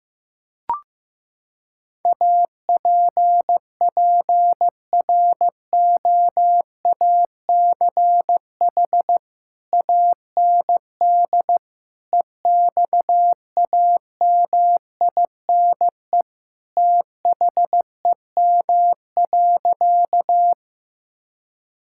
Morse code, heard as a keyed tone, about 15 wpm, 700 Hz. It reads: APPROACH AND EXAMINE THEM.
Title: Sentences from Top 1600 Words 15wpm